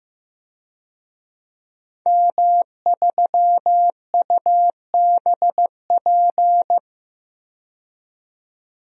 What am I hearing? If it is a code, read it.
Morse code, heard as a keyed tone, about 15 wpm, 700 Hz. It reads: M3UBP
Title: DXCC-100 Call Signs - 5 Characters - 15wpm